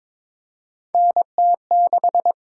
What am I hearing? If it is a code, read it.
Morse code, heard as a keyed tone, about 22 wpm, 700 Hz. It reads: NT6
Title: USA Call Signs - 4 Characters - 8x Character Spacing - 22wpm